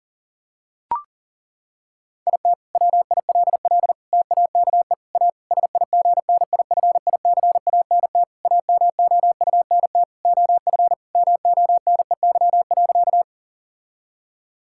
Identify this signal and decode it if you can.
Morse code, heard as a keyed tone, about 40 wpm, 700 Hz. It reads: IT WILL TAKE A SIGNIFICANT AMOUNT OF MONEY.
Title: Sentences from Top 1000 Words - Review Part 6 - 40wpm